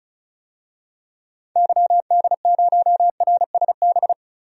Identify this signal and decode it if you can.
Morse code, heard as a keyed tone, about 36 wpm, 700 Hz. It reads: YD0RSB
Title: DXCC-100 Call Signs - 6 Characters - Repeated 3x - 35wpm